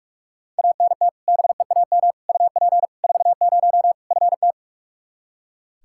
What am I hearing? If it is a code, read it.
Morse code, heard as a keyed tone, about 42 wpm, 700 Hz. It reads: ANT BEAM UP 40 FT
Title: Practice QSO - Exchange 2 - Info - 45wpm